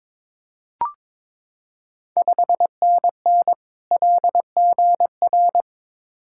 Morse code, heard as a keyed tone, about 21 wpm, 700 Hz. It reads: 5NN LGR
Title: Ontario QSO Party Exchange 22wpm